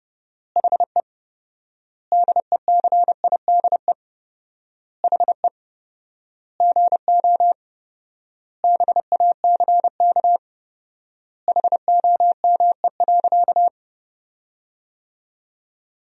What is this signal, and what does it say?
Morse code, heard as a keyed tone, about 30 wpm, 700 Hz. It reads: HE DECIDE HE GO BACK HOME.
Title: 4x Word-Spacing - Sentences from Top 500 Words - 30wpm